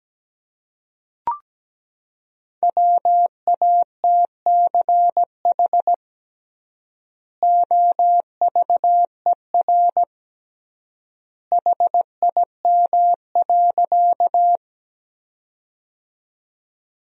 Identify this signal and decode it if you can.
Morse code, heard as a keyed tone, about 17 wpm, 700 Hz. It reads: WATCH OVER HIM.
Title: 3x Word-Spacing - Sentences from Top 300 Words - 17wpm